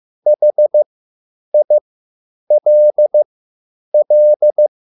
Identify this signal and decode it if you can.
Morse code, heard as a keyed tone, about 15 wpm, spaced as for 9 wpm, 600 Hz. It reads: HILL
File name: QSL / CW lesson